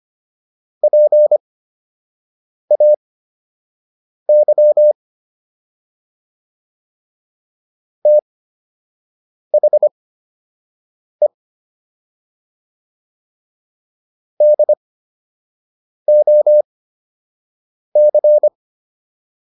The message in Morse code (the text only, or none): PAY THE DOC